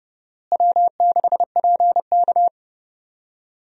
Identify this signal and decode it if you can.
Morse code, heard as a keyed tone, about 30 wpm, 700 Hz. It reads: W6PK
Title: USA Call Signs - 4 Characters - Repeated 2x - 30wpm